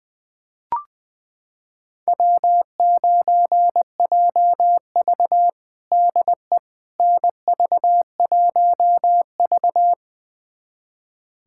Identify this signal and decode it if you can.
Morse code, heard as a keyed tone, about 21 wpm, 700 Hz. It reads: W9JV DE NV1V
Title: USA Callsign DE Callsign - 4 Characters - 20wpm